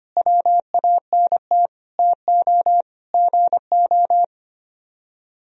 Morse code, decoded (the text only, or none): WANT TO GO